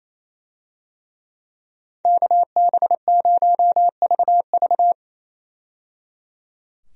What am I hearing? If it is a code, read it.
Morse code, heard as a keyed tone, about 28 wpm, 700 Hz. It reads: KB0VV
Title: USA Call Signs - 5 Characters - 28wpm